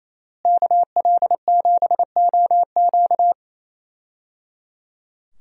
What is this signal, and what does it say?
Morse code, heard as a keyed tone, about 28 wpm, 700 Hz. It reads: KL7OQ
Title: DXCC-100 Call Signs - 5 Characters - Repeated 3x - 28wpm